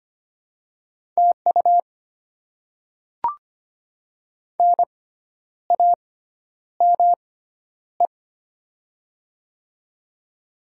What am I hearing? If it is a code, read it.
Morse code, heard as a keyed tone, about 25 wpm, spaced as for 10 wpm, 700 Hz. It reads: TU NAME
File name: Top 90 QSO Elements - 6x Character Spacing - 25wpm